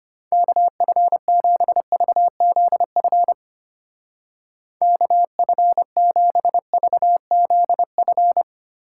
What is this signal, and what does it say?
Morse code, heard as a keyed tone, about 33 wpm, 700 Hz. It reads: KF7VZF KF7VZF